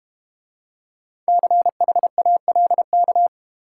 Morse code, heard as a keyed tone, about 30 wpm, 700 Hz. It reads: CHALK
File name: Easy Words - Speed-Racing with Hints - 25wpm